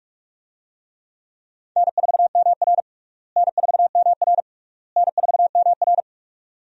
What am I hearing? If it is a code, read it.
Morse code, heard as a keyed tone, about 46 wpm, 700 Hz. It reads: N4MR N4MR N4MR
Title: USA Call Signs - 4 Characters - Repeated 3x - 45wpm